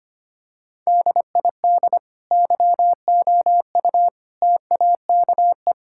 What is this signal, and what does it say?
Morse code, heard as a keyed tone, about 25 wpm, 700 Hz. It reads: DID YOU TAKE
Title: Sentences from Top 1000 Words - Review Part 5 - 25wpm